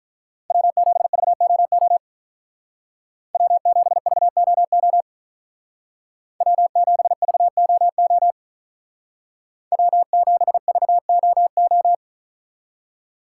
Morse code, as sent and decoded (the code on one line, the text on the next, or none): .-- --... ...- --- --- / .-- --... ...- --- --- / .-- --... ...- --- --- / .-- --... ...- --- ---
W7VOO W7VOO W7VOO W7VOO